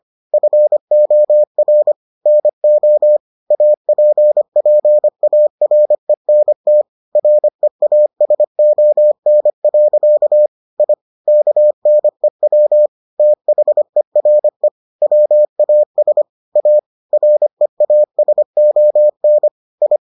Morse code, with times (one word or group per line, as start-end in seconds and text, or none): 0.3-1.9 FOR
2.3-3.2 NO
3.5-6.8 APPARENT
7.2-10.5 REASON.
10.8-10.9 I
11.3-12.9 KNEW
13.2-14.7 THERE
15.0-16.2 WAS
16.6-16.8 A
17.1-19.5 REASON
19.8-20.0 I